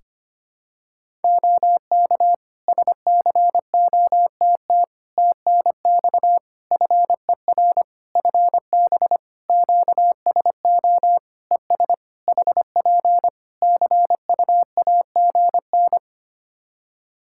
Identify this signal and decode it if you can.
Morse code, heard as a keyed tone, about 25 wpm, 700 Hz. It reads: OK SCOTT TNX FER FB QSO ES HP CUAGN
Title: Practice QSO - Exchange 4 - Ending - 25wpm